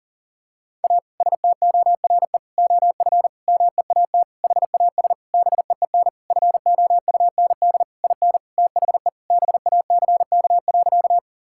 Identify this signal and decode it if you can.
Morse code, heard as a keyed tone, about 37 wpm, 700 Hz. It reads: A STORE OF MEAT HAS BEEN FOUND IN THE BACK.
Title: Sentences from Top 1000 Words - Review Part 5 - 40wpm